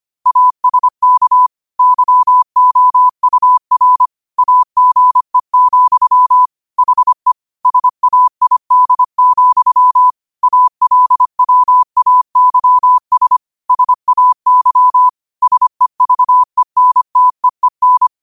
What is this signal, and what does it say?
Morse code, heard as a keyed tone, about 25 wpm, 1000 Hz. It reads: ASK YOUR AGE, HE SAID, ALWAYS SAY SEVENTEEN
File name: qotdfiles / 25wpm